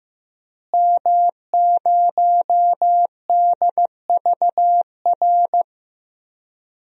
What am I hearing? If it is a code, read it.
Morse code, heard as a keyed tone, about 15 wpm, 700 Hz. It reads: M0DVR
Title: DXCC-100 Call Signs - 5 Characters - Repeated 2x - 15wpm